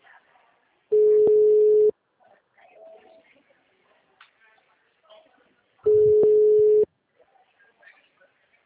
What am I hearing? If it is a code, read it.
Ringback tone (Europe, 425 Hz), with speech.